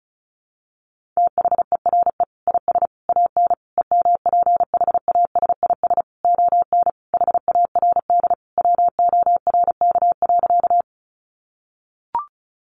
Morse code, heard as a keyed tone, about 34 wpm, 700 Hz. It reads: THERE IS AN EMPHASIS ON HARD WORK.
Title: Sentences from Top 1400 Words 35wpm